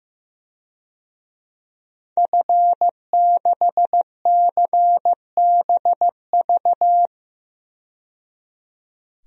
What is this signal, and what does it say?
Morse code, heard as a keyed tone, about 15 wpm, 700 Hz. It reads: F6CBV
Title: DXCC-100 Call Signs - 5 Characters - 15wpm